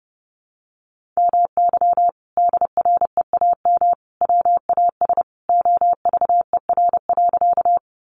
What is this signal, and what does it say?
Morse code, heard as a keyed tone, about 30 wpm, 700 Hz. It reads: MY DREAM WAS OVER.
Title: Sentences from Top 1500 Words 30wpm